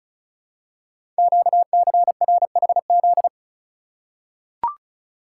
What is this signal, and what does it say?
Morse code, heard as a keyed tone, about 36 wpm, 700 Hz. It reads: QCRHZ
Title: Code Groups - 5 Letters - 35wpm